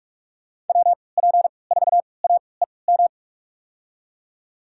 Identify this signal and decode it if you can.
Morse code, heard as a keyed tone, about 42 wpm, spaced as for 25 wpm, 700 Hz. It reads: WP4AEM
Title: USA Call Signs - 6 Characters - 3x Character Spacing - 45wpm